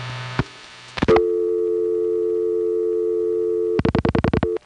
Dial tone (North America), 350 + 440 Hz.